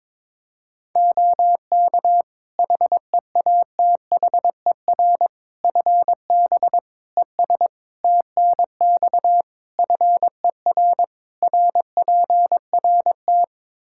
Morse code, heard as a keyed tone, about 22 wpm, 700 Hz. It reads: OK HEATHER FB ES TNX FER RPRT
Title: K1USN SST QSO Practice 22wpm